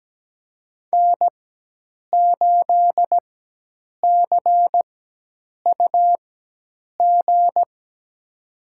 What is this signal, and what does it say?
Morse code, heard as a keyed tone, about 17 wpm, spaced as for 8 wpm, 700 Hz. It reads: N8CUG
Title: USA Call Signs - 4x Character Spacing - 17wpm